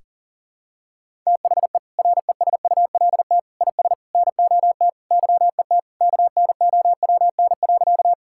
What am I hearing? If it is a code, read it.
Morse code, heard as a keyed tone, about 40 wpm, 700 Hz. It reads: THE RESULT IS NOT YET KNOWN.
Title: Sentences from Top 1000 Words - Review Part 5 - 40wpm